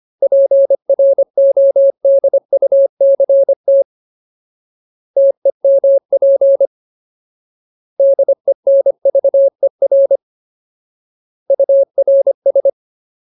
Morse code, heard as a keyed tone, about 25 wpm, 550 Hz. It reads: PRODUCT TEMP DENVER URS
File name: Ultimate Word List / 25WPM Ultimate